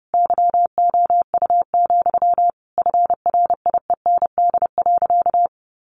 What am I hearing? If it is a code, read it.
Morse code, heard as a keyed tone, about 30 wpm, 700 Hz. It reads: YOU, FRIEND.
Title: Sentences from Top 1500 Words 30wpm